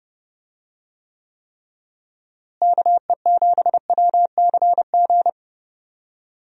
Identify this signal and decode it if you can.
Morse code, heard as a keyed tone, about 30 wpm, 700 Hz. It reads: KE7WCG
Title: USA Call Signs - 6 Characters - Repeated 3x - 30wpm